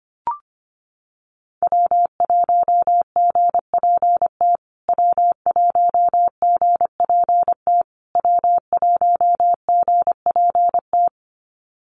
Morse code, heard as a keyed tone, about 25 wpm, 700 Hz. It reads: W1GPT W1GPT W1GPT
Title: USA Call Signs Repeated 3x 25wpm